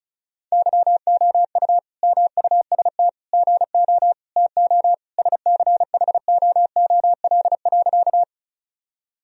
Morse code, heard as a keyed tone, about 36 wpm, 700 Hz. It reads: YOU MUST GO TO SCHOOL.
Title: Sentences from Top 1000 Words - Review Part 1 - 35wpm